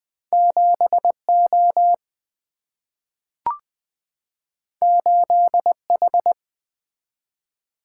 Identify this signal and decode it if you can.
Morse code, heard as a keyed tone, about 20 wpm, 700 Hz. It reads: 7O 8H
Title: Single-Letter Single-Number 20wpm